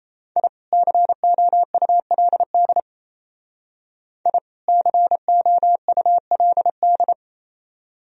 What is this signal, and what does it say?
Morse code, heard as a keyed tone, about 32 wpm, 700 Hz. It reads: I COULD I COULD
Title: Speed-Racing - Sets of 2 Words 28wpm